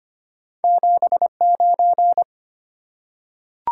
Morse code, heard as a keyed tone, about 25 wpm, 700 Hz. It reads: 79
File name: Two Letter-Number Combinations - 25wpm